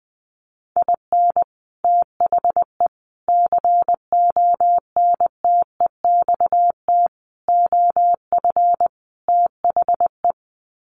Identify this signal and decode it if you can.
Morse code, heard as a keyed tone, about 20 wpm, 700 Hz. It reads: IN THE CONTEXT OF THE
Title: Sets of 5 Words 20wpm